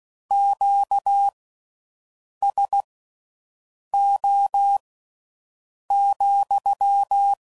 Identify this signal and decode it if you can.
Morse code, heard as a keyed tone, about 16 wpm, spaced as for 6 wpm, 800 Hz. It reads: QSO,